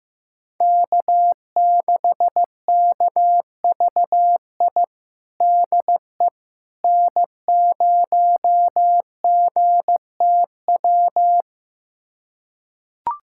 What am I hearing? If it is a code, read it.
Morse code, heard as a keyed tone, about 15 wpm, 700 Hz. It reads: K6KVI DE N0GTW
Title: USA Callsign DE Callsign - 5 Characters - 15wpm